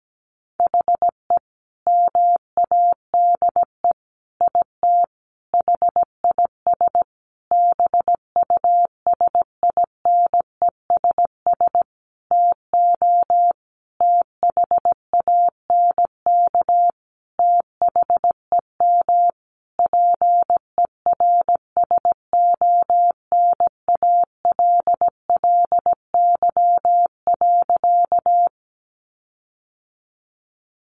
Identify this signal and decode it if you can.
Morse code, heard as a keyed tone, about 17 wpm, 700 Hz. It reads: HE MADE IT HIS BUSINESS TO THANK THEM PERSONALLY.
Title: Sentences from Top 1400 Words 17wpm